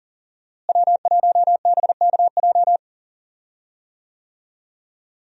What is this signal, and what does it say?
Morse code, heard as a keyed tone, about 40 wpm, 700 Hz. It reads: W1BKJ
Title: USA Call Signs - 5 Characters - Repeated 2x - 40wpm